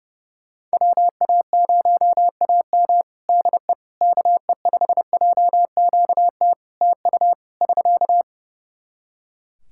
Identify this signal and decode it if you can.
Morse code, heard as a keyed tone, about 30 wpm, 700 Hz. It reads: WA0AM DE KE5JQT TU <SK>
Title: Practice QSO - Exchange 4 - Ending - 30wpm